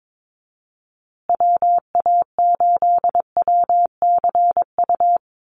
Morse code, heard as a keyed tone, about 22 wpm, 700 Hz. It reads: WA8WCU
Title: USA Call Signs Repeated 3x 22wpm